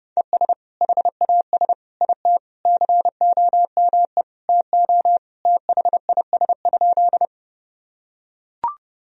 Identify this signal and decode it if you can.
Morse code, heard as a keyed tone, about 30 wpm, 700 Hz. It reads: ES HAS IT COME TO THIS?
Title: Sentences from Top 100 Words and Abbreviations 30wpm